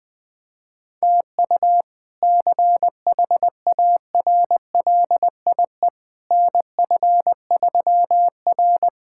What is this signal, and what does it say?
Morse code, heard as a keyed tone, about 21 wpm, 700 Hz. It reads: TU CHARLIE NF3R
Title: K1USN SST QSO Practice - 20wpm